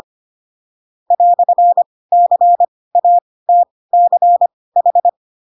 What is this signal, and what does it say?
Morse code, heard as a keyed tone, about 25 wpm, spaced as for 18 wpm, 700 Hz. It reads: "CATCH
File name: Pinocchio ch3 12 25 ik1hge